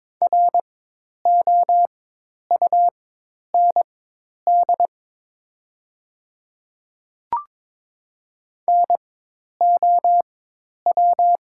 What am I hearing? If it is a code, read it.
Morse code, heard as a keyed tone, about 22 wpm, spaced as for 10 wpm, 700 Hz. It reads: ROUND NOW
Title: 4x Character Spacing - Top 200 Words - 22wpm